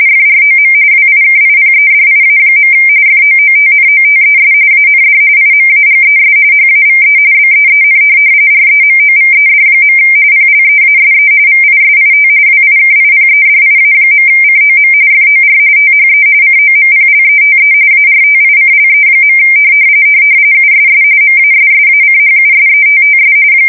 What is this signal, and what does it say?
RTTY, about 45.45 baud. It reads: FROM WIKIPEDIA. FOOTNOTES AND IMAGES HAVE BEEN REMOVED. PLEASE VISIT WIKIPEDIA.COM FOR THE COMPLETE ARTICLE. USS MASSACHUSETTS